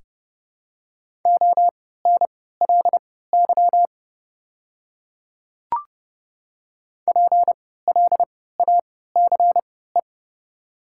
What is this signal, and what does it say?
Morse code, heard as a keyed tone, about 30 wpm, spaced as for 17 wpm, 700 Hz. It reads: ONLY PLACE
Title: 3x Character Spacing - Top 200 Words - 30wpm